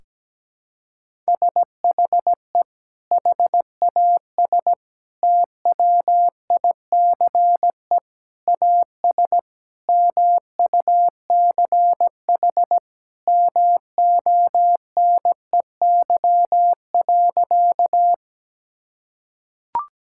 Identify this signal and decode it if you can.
Morse code, heard as a keyed tone, about 17 wpm, 700 Hz. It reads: SHE HAS TWICE AS MUCH MONEY.
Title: Sentences from Top 1600 Words 17wpm